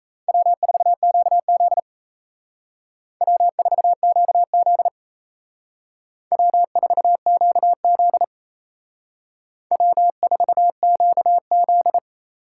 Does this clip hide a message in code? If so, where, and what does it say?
Yes: Morse code from 0.3 to 12.0 s, heard as a keyed tone, about 35 wpm, 700 Hz. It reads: W4QZ W4QZ W4QZ W4QZ